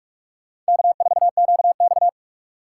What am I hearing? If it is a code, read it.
Morse code, heard as a keyed tone, about 57 wpm, 700 Hz. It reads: K4QX